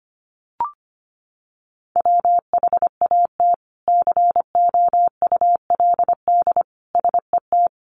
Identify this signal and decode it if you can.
Morse code, heard as a keyed tone, about 25 wpm, 700 Hz. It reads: WHAT COULD SET
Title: Sentences from Top 200 Words 25wpm